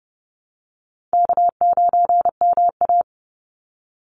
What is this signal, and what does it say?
Morse code, heard as a keyed tone, about 30 wpm, 700 Hz. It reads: K9MA
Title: Easy Call Signs Repeated 2x - 30wpm